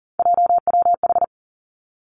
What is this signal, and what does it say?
Morse code, heard as a keyed tone, about 38 wpm, 690 Hz. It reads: JWH
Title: Three Challenging Letter-Number Combinations 40wpm